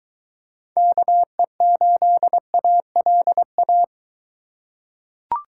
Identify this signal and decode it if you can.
Morse code, heard as a keyed tone, about 20 wpm, 700 Hz. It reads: KE8ALA